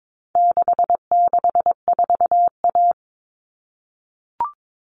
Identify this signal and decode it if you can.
Morse code, heard as a keyed tone, about 22 wpm, 700 Hz. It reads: 664A
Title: All Cut Numbers - 4-Digit 22wpm